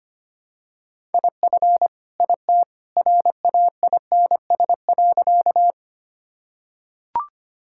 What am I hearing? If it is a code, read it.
Morse code, heard as a keyed tone, about 25 wpm, 700 Hz. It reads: IF IT RAINS.
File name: practice / 25wpm / Sentences with 2-Word Phrases - 25wpm